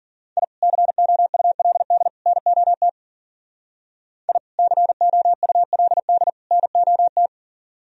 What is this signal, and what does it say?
Morse code, heard as a keyed tone, about 46 wpm, 700 Hz. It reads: I COULD NOT I COULD NOT
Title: Speed-Racing - Sets of 3 Words 40wpm